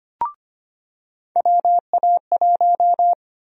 Morse code, heard as a keyed tone, about 25 wpm, 700 Hz. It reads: WA1
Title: USA Call Sign Prefixes - 25wpm